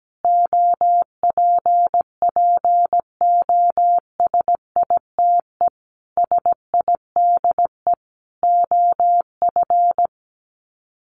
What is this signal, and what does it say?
Morse code, heard as a keyed tone, about 17 wpm, 700 Hz. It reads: OPPOSITE SIDE OF
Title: Sets of 5 Words Encore2 17wpm